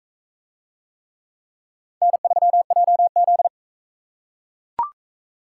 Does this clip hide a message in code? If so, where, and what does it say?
Yes: Morse code from 2.0 to 3.5 s, heard as a keyed tone, about 39 wpm, 700 Hz. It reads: N3JZ